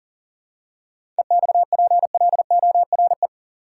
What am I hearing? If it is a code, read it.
Morse code, heard as a keyed tone, about 40 wpm, 700 Hz. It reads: EXPLORE
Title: Seven Letter Words - 40wpm